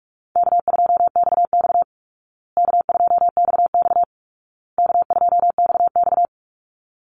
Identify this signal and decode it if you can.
Morse code, heard as a keyed tone, about 44 wpm, 700 Hz. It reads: K2XX K2XX K2XX
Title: Easy Call Signs Repeated 3x - 45wpm